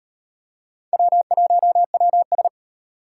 Morse code, heard as a keyed tone, about 36 wpm, 700 Hz. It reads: W1WS